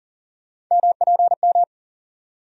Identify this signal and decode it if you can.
Morse code, heard as a keyed tone, about 40 wpm, 700 Hz. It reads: MPM
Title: USA Call Sign Postfixes - 3 Letters 40wpm